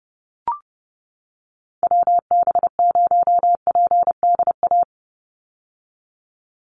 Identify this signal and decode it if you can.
Morse code, heard as a keyed tone, about 30 wpm, 700 Hz. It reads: WB0PDA
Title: USA Call Signs 30wpm